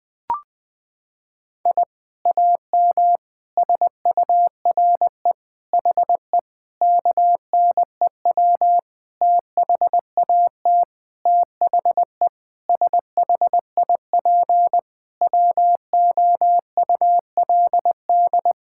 Morse code, heard as a keyed tone, about 21 wpm, 700 Hz. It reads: I AM SURE HE KNEW THAT THE SHIP WOULD
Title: Sentences from Top 1000 Words - Review Part 6 - 20wpm